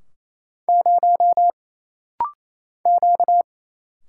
Morse code, heard as a keyed tone, about 28 wpm, 700 Hz. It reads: 0 Q
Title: Single Letter-Number - Rapid-Fire 28wpm